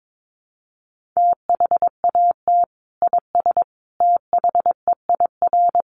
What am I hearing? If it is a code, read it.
Morse code, heard as a keyed tone, about 22 wpm, 700 Hz. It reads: THAT IS THEIR
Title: Sentences from Top 1400 Words 22wpm